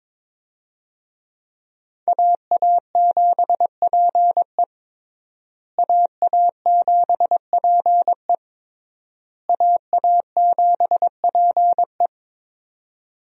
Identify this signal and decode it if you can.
Morse code, heard as a keyed tone, about 22 wpm, 700 Hz. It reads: AA7PE AA7PE AA7PE
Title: USA Call Signs - 5 Characters - Repeated 3x - 22wpm